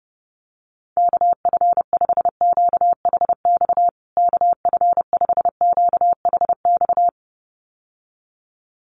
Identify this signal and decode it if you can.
Morse code, heard as a keyed tone, about 30 wpm, 700 Hz. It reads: KF5QHX KF5QHX
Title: USA Call Signs Repeated 2x 30wpm